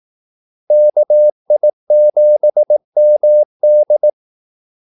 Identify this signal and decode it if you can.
Morse code, heard as a keyed tone, about 18 wpm, 600 Hz. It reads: KI7MD
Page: Kaldesignaler 18 wpm